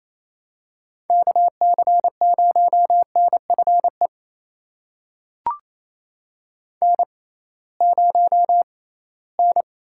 Morse code, heard as a keyed tone, about 28 wpm, spaced as for 18 wpm, 700 Hz. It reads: KC0NFE N0N
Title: USA Call Signs - 6x Character Spacing - 28wpm